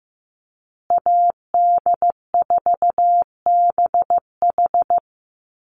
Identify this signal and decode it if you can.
Morse code, heard as a keyed tone, about 14 wpm, 700 Hz. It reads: AD4BH
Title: USA Call Signs Repeated 2x 15wpm